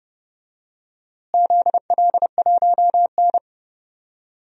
Morse code, heard as a keyed tone, about 30 wpm, 700 Hz. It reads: ZL1N
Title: DXCC-100 Call Signs - 4 Characters - Repeated 3x - 30wpm